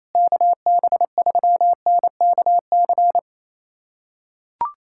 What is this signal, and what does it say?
Morse code, heard as a keyed tone, about 28 wpm, 700 Hz. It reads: KB3NKC
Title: USA Callsign DE Callsign - 6 Characters - 28wpm